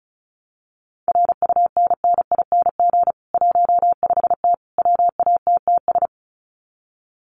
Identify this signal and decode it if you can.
Morse code, heard as a keyed tone, about 34 wpm, 700 Hz. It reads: RUNNING 15T WATTS
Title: Common QSO Phrases 35wpm